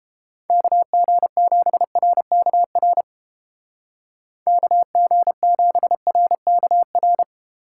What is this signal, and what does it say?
Morse code, heard as a keyed tone, about 29 wpm, 700 Hz. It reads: KG7RKR KG7RKR